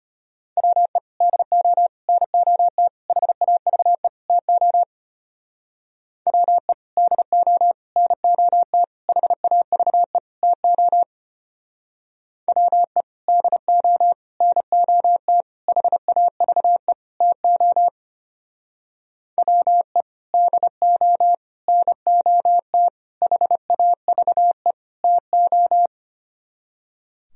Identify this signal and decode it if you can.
Morse code, heard as a keyed tone, about 32 wpm, 700 Hz. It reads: WE DO NOT HAVE TO WE DO NOT HAVE TO WE DO NOT HAVE TO WE DO NOT HAVE TO